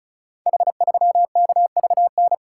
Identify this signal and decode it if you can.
Morse code, heard as a keyed tone, about 34 wpm, 700 Hz. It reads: H3KVN
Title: Code Groups - 5 Letter:Numbers - 35wpm